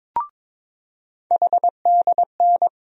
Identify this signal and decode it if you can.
Morse code, heard as a keyed tone, about 22 wpm, 700 Hz. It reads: HDN
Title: USA Call Sign Postfixes - 3 Letters 22wpm